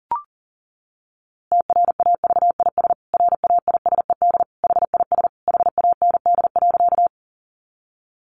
Morse code, heard as a keyed tone, about 37 wpm, 700 Hz. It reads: TRAVIS RAISED HIS HAND.
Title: A Sound Of Thunder 40wpm